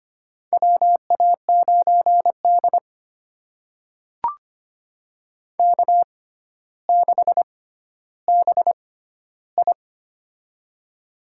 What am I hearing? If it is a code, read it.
Morse code, heard as a keyed tone, about 25 wpm, spaced as for 13 wpm, 700 Hz. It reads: WA9D K6BI
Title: USA Call Signs - 4 Characters - 6x Character Spacing - 25wpm